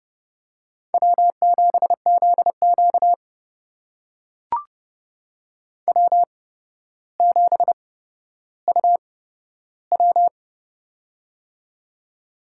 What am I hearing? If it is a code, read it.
Morse code, heard as a keyed tone, about 30 wpm, spaced as for 13 wpm, 700 Hz. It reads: W7ZQ W7UW
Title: USA Call Signs - 4 Characters - 8x Character Spacing - 30wpm